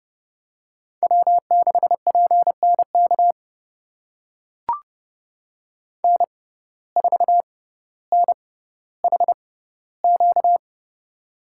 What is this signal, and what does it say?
Morse code, heard as a keyed tone, about 30 wpm, spaced as for 15 wpm, 700 Hz. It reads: W6PNK N4NHQ
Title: USA Call Signs - 5 Characters - 6x Character Spacing - 30wpm